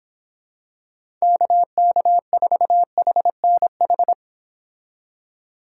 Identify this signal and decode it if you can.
Morse code, heard as a keyed tone, about 25 wpm, 700 Hz. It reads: KK4HNH
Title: Speed Racing - USA 2x3 Call Signs 22wpm